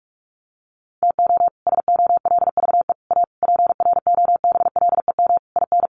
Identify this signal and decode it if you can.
Morse code, heard as a keyed tone, about 44 wpm, 700 Hz. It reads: TO SOLVE A PROBLEM IN
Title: Sentences from Top 1400 Words 45wpm